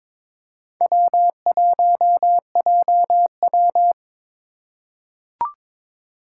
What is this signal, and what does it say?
Morse code, heard as a keyed tone, about 22 wpm, 700 Hz. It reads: W1JW
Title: USA Call Signs - 3x Character Spacing - 22wpm